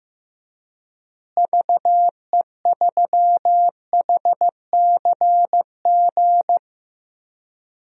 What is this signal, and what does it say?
Morse code, heard as a keyed tone, about 15 wpm, 700 Hz. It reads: VE3HCG
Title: DXCC-100 Call Signs - 6 Characters - Repeated 3x - 15wpm